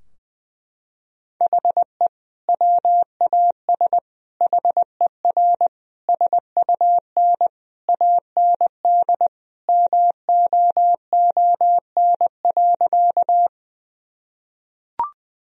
Morse code, heard as a keyed tone, about 21 wpm, 700 Hz. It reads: HE WAS HER SUN AND MOON.
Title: Sentences from Top 1000 Words - Review Part 3 - 20wpm